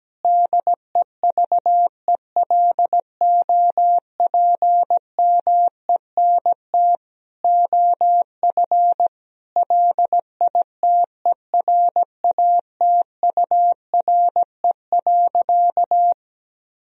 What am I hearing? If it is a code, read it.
Morse code, heard as a keyed tone, about 17 wpm, 700 Hz. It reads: DEVELOPMENT OF LITERATURE.
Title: Sentences from Top 1000 Words - Review Part 4 - 17wpm